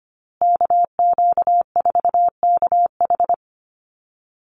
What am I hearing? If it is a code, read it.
Morse code, heard as a keyed tone, about 25 wpm, 700 Hz. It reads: KQ4KH
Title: USA Call Signs Repeated 2x 25wpm